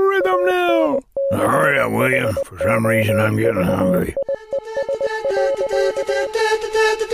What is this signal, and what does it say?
Morse code, heard as a keyed tone, about 20 wpm, 550 Hz, with speech. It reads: ANCHOVIES.